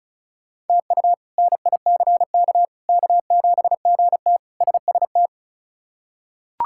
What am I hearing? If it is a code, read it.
Morse code, heard as a keyed tone, about 36 wpm, 700 Hz. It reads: TU NICK K7GT SST
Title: K1USN SST QSO Practice - 35wpm